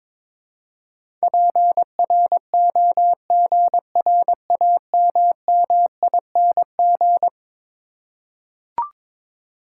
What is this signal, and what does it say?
Morse code, heard as a keyed tone, about 22 wpm, 700 Hz. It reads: PROGRAMMING
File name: practice / 22wpm / Eleven Letter Words - 22wpm